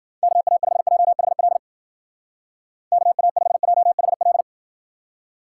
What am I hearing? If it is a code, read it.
Morse code, heard as a keyed tone, about 69 wpm, 700 Hz. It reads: KA5JHL KA5JHL